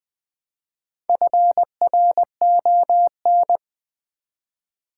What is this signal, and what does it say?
Morse code, heard as a keyed tone, about 20 wpm, 700 Hz. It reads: FRON
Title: Build A Word - 1 Letter At A Time - 5-Letter Words - 20wpm